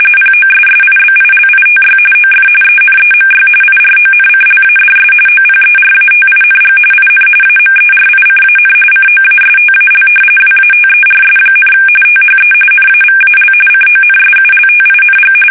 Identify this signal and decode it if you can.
RTTY, about 45.45 baud. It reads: POSSIBILITY OF ENEMY IRONCLADS OPERATING IN AMERICAN COASTAL WATERS, THE NAVAL CONSULTING
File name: fsk850 / battleships